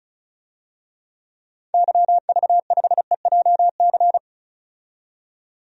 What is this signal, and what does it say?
Morse code, heard as a keyed tone, about 34 wpm, 700 Hz. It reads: YV5EJC
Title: DXCC-100 Call Signs - 6 Characters - 35wpm